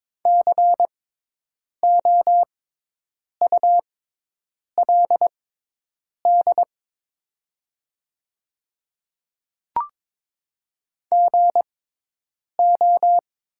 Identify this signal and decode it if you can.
Morse code, heard as a keyed tone, about 21 wpm, spaced as for 8 wpm, 690 Hz. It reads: COULD GO
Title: 6x Character Spacing - Top 100 Words - 22wpm